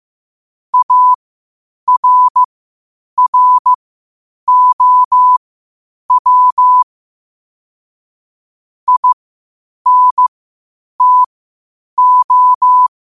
Morse code, heard as a keyed tone, about 15 wpm, spaced as for 8 wpm, 1000 Hz. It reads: ARROW INTO
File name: qotdfiles / 8wpm